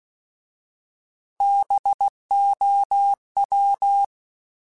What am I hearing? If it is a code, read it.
Morse code, heard as a keyed tone, about 16 wpm, 800 Hz. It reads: BOW
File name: Alphabetical / Alphabetical mp3s / Bs @15